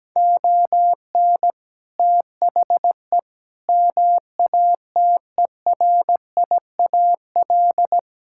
Morse code, heard as a keyed tone, about 17 wpm, 700 Hz. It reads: ON THE MATERIAL
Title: Sentences from Top 1000 Words - Review Part 5 - 17wpm